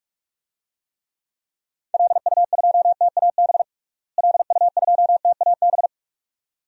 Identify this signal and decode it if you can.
Morse code, heard as a keyed tone, about 41 wpm, 700 Hz. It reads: LU2TAB LU2TAB
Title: DXCC-100 Call Signs - 6 Characters - Repeated 2x - 45wpm